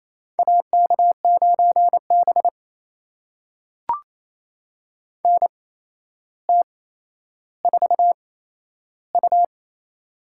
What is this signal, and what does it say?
Morse code, heard as a keyed tone, about 28 wpm, spaced as for 12 wpm, 700 Hz. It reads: AK9B NT4U
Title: USA Call Signs - 4 Characters - 8x Character Spacing - 28wpm